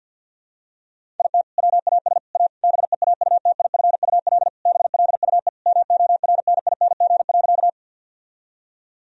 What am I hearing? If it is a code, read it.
Morse code, heard as a keyed tone, about 46 wpm, 700 Hz. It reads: IT WAS A BEAUTIFUL BLUE MORNING.
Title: Sentences from Top 1000 Words - Review Part 3 - 50wpm